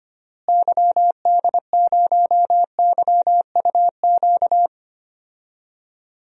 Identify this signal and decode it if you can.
Morse code, heard as a keyed tone, about 25 wpm, 700 Hz. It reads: YD0YUQ
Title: DXCC-100 Call Signs - 6 Characters - Repeated 2x - 25wpm